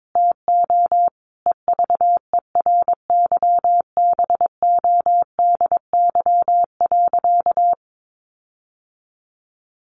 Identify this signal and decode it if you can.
Morse code, heard as a keyed tone, about 22 wpm, 700 Hz. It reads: TO EVERYBODY.
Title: Sentences from Top 1400 Words 22wpm